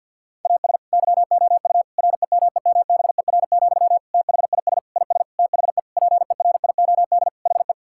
Morse code, heard as a keyed tone, about 50 wpm, 700 Hz. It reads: AS YOU REMEMBER, THIS IS THE PERIOD HE
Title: Sentences from Top 1000 Words - Review Part 5 - 50wpm